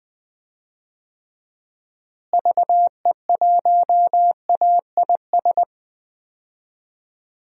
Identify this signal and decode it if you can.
Morse code, heard as a keyed tone, about 20 wpm, 700 Hz. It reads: VE1AIS
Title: DXCC-100 Call Signs - 6 Characters - Repeated 2x - 20wpm